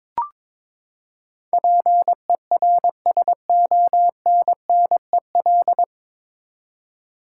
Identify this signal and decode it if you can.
Morse code, heard as a keyed tone, about 22 wpm, 700 Hz. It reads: PERSONNEL
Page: Morse Code Ninja practice 22wpm